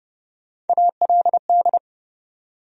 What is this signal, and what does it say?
Morse code, heard as a keyed tone, about 30 wpm, 700 Hz. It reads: ALD
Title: USA Call Sign Postfixes - 3 Letters 30wpm